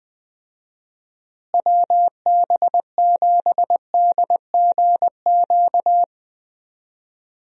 Morse code, heard as a keyed tone, about 20 wpm, 700 Hz. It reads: WB7DGQ
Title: USA Call Signs - 6 Characters - Repeated 2x - 20wpm